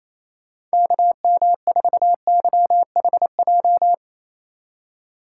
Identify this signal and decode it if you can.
Morse code, heard as a keyed tone, about 28 wpm, 700 Hz. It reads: KM4YHJ
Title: USA Call Signs - 6 Characters - 3x Character Spacing - 28wpm